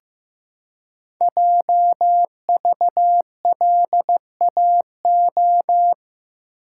Morse code, heard as a keyed tone, about 15 wpm, 700 Hz. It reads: JVLAO
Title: Code Groups - 5 Letter:Numbers - 15wpm